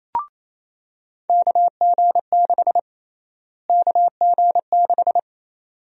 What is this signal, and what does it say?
Morse code, heard as a keyed tone, about 28 wpm, 700 Hz. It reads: KG6 KG6
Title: USA Call Sign Prefixes - Repeated 2x - 28wpm